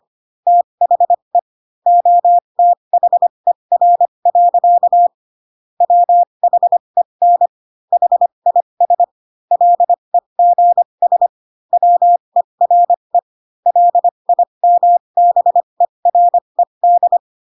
Morse code, heard as a keyed tone, about 25 wpm, spaced as for 22 wpm, 700 Hz. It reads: THE OTHER. WHEN HIS LEGS WERE LIMBERED
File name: Pinocchio / Pinocchio ch3 18 25 ik1hge